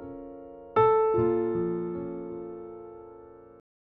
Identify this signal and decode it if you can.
Dial tone (North America), 350 + 440 Hz.